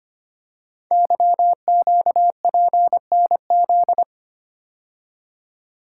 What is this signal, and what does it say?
Morse code, heard as a keyed tone, about 25 wpm, 700 Hz. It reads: YQPNZ
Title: Code Groups - 5 Letters - 25wpm